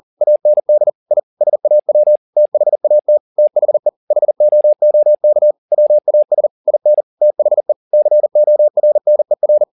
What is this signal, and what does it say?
Morse code, heard as a keyed tone, about 40 wpm, 600 Hz. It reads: AND I SAW THAT THE HOOK WAS IN THE CORNER
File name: the old man and the sea 2 40wpm